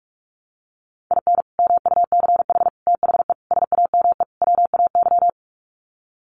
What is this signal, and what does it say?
Morse code, heard as a keyed tone, about 44 wpm, 700 Hz. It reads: IN MUCH THE SAME WAY
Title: Sets of 5 Words Encore 45wpm